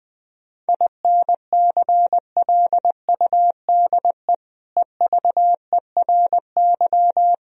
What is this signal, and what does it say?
Morse code, heard as a keyed tone, about 20 wpm, 700 Hz. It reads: INCLUDE EVERY
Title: Sentences from Top 1000 Words - Review Part 4 - 20wpm